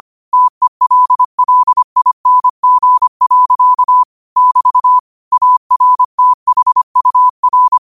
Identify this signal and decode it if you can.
Morse code, heard as a keyed tone, about 25 wpm, 1000 Hz. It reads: TELLING. <BT> ARTHUR
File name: qotdfiles / 25wpm